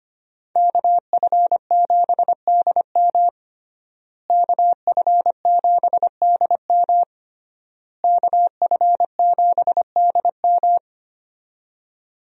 Morse code, heard as a keyed tone, about 25 wpm, 700 Hz. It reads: KF7DM KF7DM KF7DM
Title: USA Call Signs - 5 Characters - Repeated 3x - 25wpm